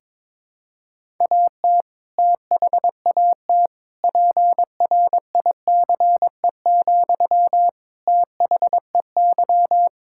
Morse code, heard as a keyed tone, about 22 wpm, 700 Hz. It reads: AT THAT PRICE, THEY
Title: Sentences from Top 1000 Words - Review Part 5 - 22wpm